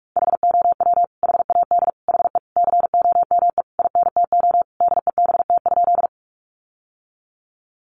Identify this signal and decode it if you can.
Morse code, heard as a keyed tone, about 44 wpm, 700 Hz. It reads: HOW HAD HE COME INTO DEBT?
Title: Sentences from Top 1300 Words 45wpm